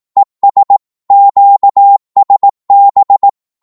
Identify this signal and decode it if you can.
Morse code, heard as a keyed tone, about 18 wpm, 800 Hz. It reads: ES QSB
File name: Morse / qso-18wpm